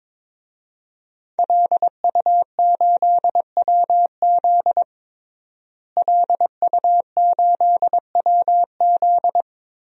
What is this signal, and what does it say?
Morse code, heard as a keyed tone, about 22 wpm, 700 Hz. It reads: LU8WZ LU8WZ
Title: DXCC-100 Call Signs - 5 Characters - Repeated 3x - 22wpm